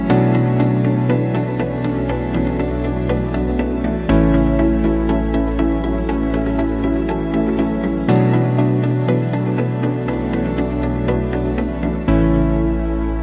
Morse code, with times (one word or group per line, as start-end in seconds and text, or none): none